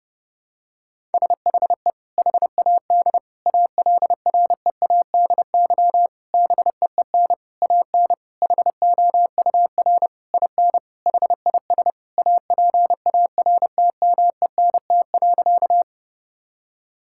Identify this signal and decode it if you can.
Morse code, heard as a keyed tone, about 32 wpm, 700 Hz. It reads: SHE HAD ALREADY BEEN AN HOUR IN HIS APARTMENT.
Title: Sentences from Top 1000 Words - Review Part 6 - 30wpm